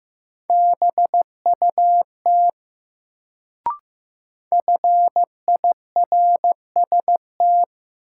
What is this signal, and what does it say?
Morse code, heard as a keyed tone, about 15 wpm, 700 Hz. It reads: BUT FIRST
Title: Sets of 2 Words - New Word Drill - 15wpm